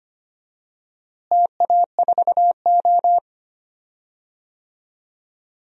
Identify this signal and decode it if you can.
Morse code, heard as a keyed tone, about 25 wpm, 700 Hz. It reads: TA4O
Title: DXCC-100 Call Signs - 4 Characters - Repeated 3x - 25wpm